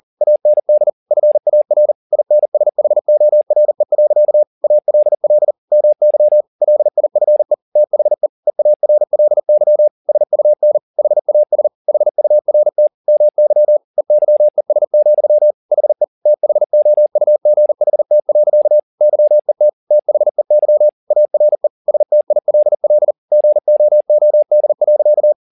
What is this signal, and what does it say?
Morse code, heard as a keyed tone, about 40 wpm, 600 Hz. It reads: AND FAR INSHORE. ALL MY LIFE THE EARLY SUN HAS HURT MY EYES, HE THOUGHT. YET THEY ARE STILL GOOD.